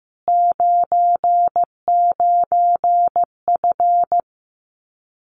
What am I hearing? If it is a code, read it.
Morse code, heard as a keyed tone, about 15 wpm, 700 Hz. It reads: 99F
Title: Common QSO Phrases Encore 15wpm